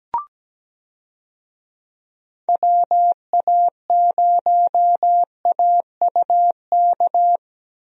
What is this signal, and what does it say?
Morse code, heard as a keyed tone, about 17 wpm, 700 Hz. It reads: WA0AUK
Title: USA Call Signs - 6 Characters - Repeated 2x - 17wpm